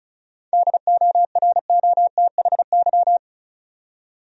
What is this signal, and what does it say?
Morse code, heard as a keyed tone, about 36 wpm, 700 Hz. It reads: DOROTHY
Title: USA Common Names - 35wpm